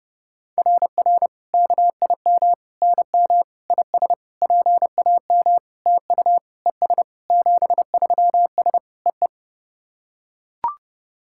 Morse code, heard as a keyed tone, about 30 wpm, 690 Hz. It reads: RR KIM NM IS PAM TU ES 73S EE
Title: SOTA QSO Practice 30wpm